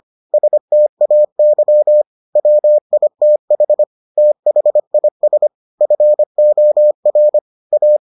Morse code, heard as a keyed tone, about 25 wpm, 600 Hz. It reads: STAY WITH THIS FOR A